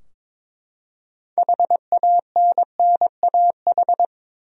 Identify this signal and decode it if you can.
Morse code, heard as a keyed tone, about 22 wpm, 700 Hz. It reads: HANNAH
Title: USA Common Names - 22wpm